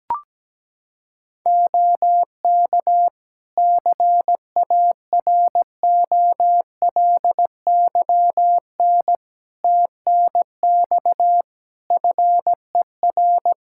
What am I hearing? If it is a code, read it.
Morse code, heard as a keyed tone, about 17 wpm, 700 Hz. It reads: OK CAROLYN TNX FER
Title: Practice QSO - Exchange 4 - Ending - 17wpm